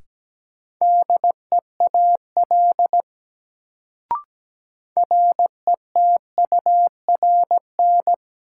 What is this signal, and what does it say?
Morse code, heard as a keyed tone, about 17 wpm, 700 Hz. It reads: DEAL RETURN
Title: Sentences from Top 800 Words - New Word Drill - 17wpm